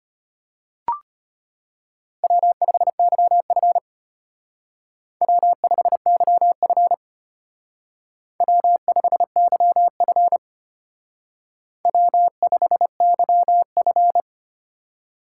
Morse code, heard as a keyed tone, about 31 wpm, 700 Hz. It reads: W5YF W5YF W5YF W5YF
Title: Speed Racing - USA 1x2 Call Signs 25wpm